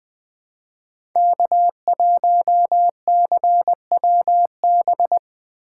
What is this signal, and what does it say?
Morse code, heard as a keyed tone, about 20 wpm, 700 Hz. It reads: K1CWB
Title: 2023 POTA Hunter Call Signs - 20wpm